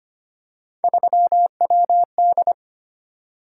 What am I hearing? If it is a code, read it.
Morse code, heard as a keyed tone, about 25 wpm, 700 Hz. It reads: 3WD
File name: Three Letter-Number Combinations - 25wpm